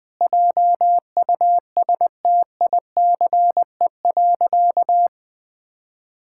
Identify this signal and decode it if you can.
Morse code, heard as a keyed tone, about 20 wpm, 700 Hz. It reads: JUSTICE.
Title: Sentences with 4-Word Phrases - 20wpm